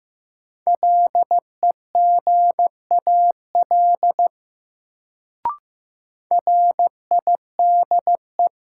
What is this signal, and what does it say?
Morse code, heard as a keyed tone, about 15 wpm, 700 Hz. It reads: LEGAL RIDE
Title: Sentences from Top 1200 Words - New Word Drill Alt - 15wpm